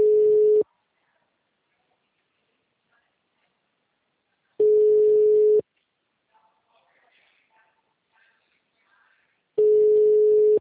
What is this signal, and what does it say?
Ringback tone (Europe, 425 Hz), with speech.